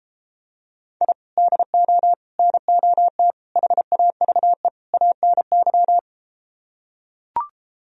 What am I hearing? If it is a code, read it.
Morse code, heard as a keyed tone, about 33 wpm, 700 Hz. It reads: I DO NOT HAVE ANY